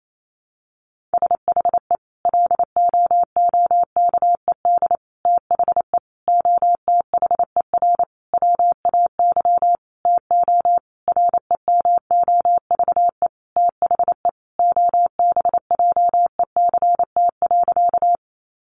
Morse code, heard as a keyed tone, about 28 wpm, 700 Hz. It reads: SHE LOOKED THE OTHER WAY TO REMOVE THE OBJECT.
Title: Sentences from Top 800 Words 28wpm